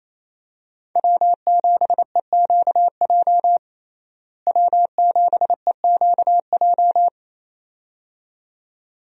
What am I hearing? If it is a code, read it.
Morse code, heard as a keyed tone, about 28 wpm, 700 Hz. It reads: W7EQJ W7EQJ
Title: USA Call Signs - 5 Characters - Repeated 2x - 28wpm